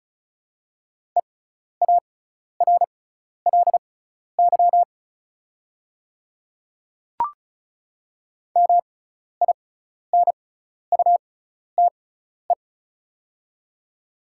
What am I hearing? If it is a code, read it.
Morse code, heard as a keyed tone, about 36 wpm, spaced as for 12 wpm, 690 Hz. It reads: EARLY MINUTE
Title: 6x Character Spacing - Top 500 Words - 35wpm